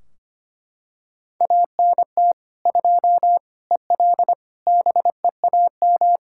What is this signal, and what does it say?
Morse code, heard as a keyed tone, about 25 wpm, 700 Hz. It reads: ANT 2 EL BEAM
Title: Practice QSO - Exchange 2 - Info - 25wpm